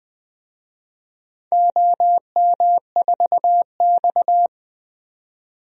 Morse code, heard as a keyed tone, about 20 wpm, 700 Hz. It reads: OM4X
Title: DXCC-100 Call Signs - 4 Characters - 20wpm